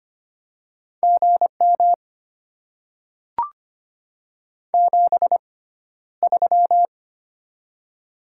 Morse code, heard as a keyed tone, about 25 wpm, spaced as for 13 wpm, 700 Hz. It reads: GM 73
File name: Top 20 QSO Elements - 6x Character Spacing - 25wpm